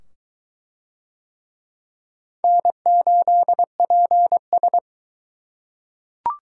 Morse code, heard as a keyed tone, about 20 wpm, 700 Hz. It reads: N8PS